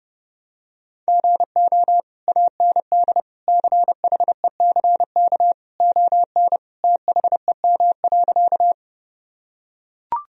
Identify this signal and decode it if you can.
Morse code, heard as a keyed tone, about 30 wpm, 700 Hz. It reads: GO AND CHECK ON THEM.
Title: Sentences from Top 1000 Words - Review Part 3 - 30wpm